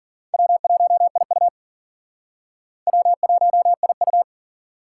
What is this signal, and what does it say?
Morse code, heard as a keyed tone, about 47 wpm, 700 Hz. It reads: W1IU W1IU
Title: Speed Racing - USA Call Signs 40wpm